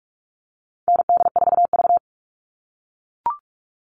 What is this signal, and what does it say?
Morse code, heard as a keyed tone, about 44 wpm, 700 Hz. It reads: ND4V
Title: All Cut Numbers - 4-Digit 45wpm